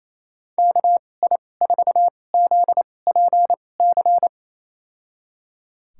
Morse code, heard as a keyed tone, about 28 wpm, spaced as for 20 wpm, 700 Hz. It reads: KI4ZPC
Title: USA Call Signs - 6 Characters - 2x Character Spacing - 28wpm